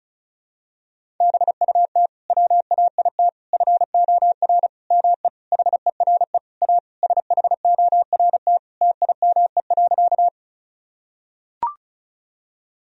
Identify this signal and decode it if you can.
Morse code, heard as a keyed tone, about 36 wpm, 700 Hz. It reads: BUT WAIT FOR ME HERE A SHORT TIME.
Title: Sentences from Top 1000 Words - Review Part 2 - 35wpm